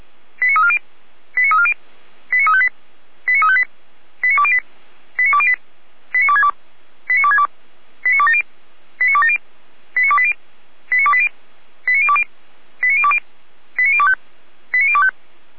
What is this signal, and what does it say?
Five-tone selective call (ZVEI), heard as quick runs of five tones, tones of about 70 ms. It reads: 89340 89340 89358 89358 89208 89208 89262 89262 89270 89270 89280 89280 89020 89020 89025 89025
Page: /455/SDR/ZVEI/